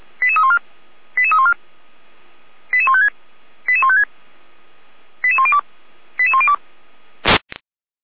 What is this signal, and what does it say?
Five-tone selective call (ZVEI), heard as quick runs of five tones, tones of about 80 ms. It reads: 88314 88314 88156 88156 88102 88102